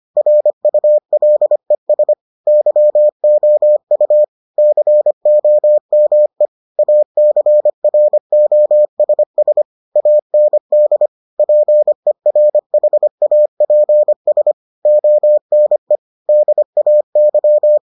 Morse code, heard as a keyed tone, about 25 wpm, 600 Hz. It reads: RULES YOU COME ACROSS AND PERHAPS ONE DAY